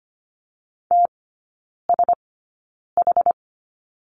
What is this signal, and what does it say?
Morse code, heard as a keyed tone, about 25 wpm, spaced as for 9 wpm, 690 Hz. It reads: TSH
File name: Lesson 14e - Number 2 - Three Characters 25wpm